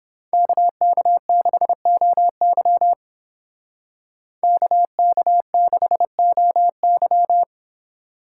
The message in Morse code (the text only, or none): KK6OY KK6OY